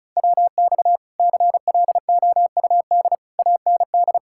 Morse code, heard as a keyed tone, about 33 wpm, 700 Hz. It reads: WX CLOUD AND
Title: Speed-Racing Common QSO Phrases - 35wpm